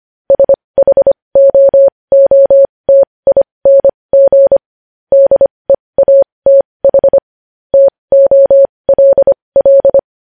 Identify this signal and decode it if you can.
Morse code, heard as a keyed tone, about 25 wpm, spaced as for 20 wpm, 550 Hz. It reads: SHOOTING DEATH TOLL
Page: news-headlines-in-morse-code-at-20-wpm